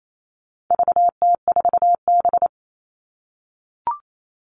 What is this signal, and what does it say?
Morse code, heard as a keyed tone, about 28 wpm, 700 Hz. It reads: VT4B
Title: All Cut Numbers - 4-Digit 28wpm